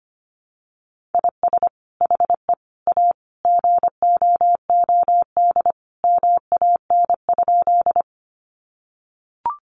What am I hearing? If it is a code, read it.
Morse code, heard as a keyed tone, about 25 wpm, 700 Hz. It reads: IS HE A GOOD MAN?
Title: Sentences from Top 200 Words 25wpm